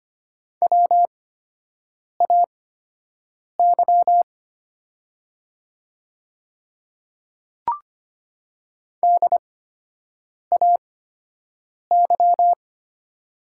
Morse code, heard as a keyed tone, about 25 wpm, spaced as for 7 wpm, 700 Hz. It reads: WAY DAY
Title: 8x Character Spacing - Top 100 Words - 25wpm